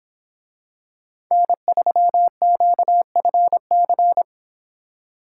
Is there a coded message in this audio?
Yes: Morse code from 1.3 to 4.2 s, heard as a keyed tone, about 26 wpm, 700 Hz. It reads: N3QFC